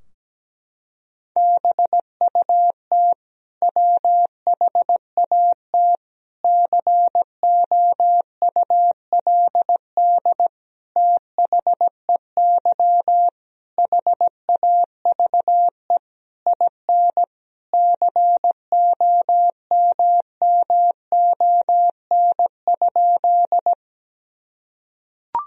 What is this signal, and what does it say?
Morse code, heard as a keyed tone, about 17 wpm, 700 Hz. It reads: BUT WHAT COULD THEY HAVE IN COMMON?
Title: Sentences from Top 1000 Words - Review Part 3 - 17wpm